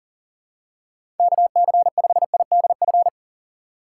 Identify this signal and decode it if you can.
Morse code, heard as a keyed tone, about 38 wpm, 700 Hz. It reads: KC5IDF
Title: USA Call Signs - 6 Characters - 4x Character Spacing - 40wpm